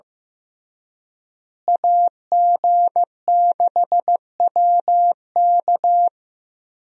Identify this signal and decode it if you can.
Morse code, heard as a keyed tone, about 15 wpm, 700 Hz. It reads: AG6WK
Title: USA Call Signs - 5 Characters - Repeated 3x - 15wpm